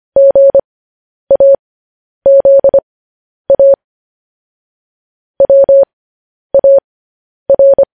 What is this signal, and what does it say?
Morse code, heard as a keyed tone, about 25 wpm, spaced as for 10 wpm, 550 Hz. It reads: GAZA WAR
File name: podcast / News headlines-1x-25x10